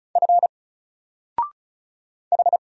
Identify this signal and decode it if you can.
Morse code, heard as a keyed tone, about 36 wpm, 700 Hz. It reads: F H
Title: Single Character (Letter:Number:Punctuation:Prosign) 35wpm